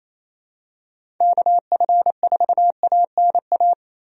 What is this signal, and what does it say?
Morse code, heard as a keyed tone, about 28 wpm, 700 Hz. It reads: KF4ANA
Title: USA Call Signs - 6 Characters - Repeated 3x - 28wpm